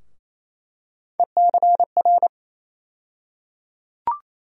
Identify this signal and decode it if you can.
Morse code, heard as a keyed tone, about 28 wpm, 700 Hz. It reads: ECR
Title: USA Call Sign Postfixes - 3 Letters 28wpm